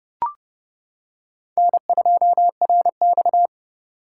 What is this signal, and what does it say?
Morse code, heard as a keyed tone, about 28 wpm, 700 Hz. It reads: N2RX